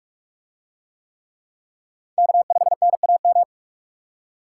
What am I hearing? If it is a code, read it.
Morse code, heard as a keyed tone, about 44 wpm, 700 Hz. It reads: K5NAM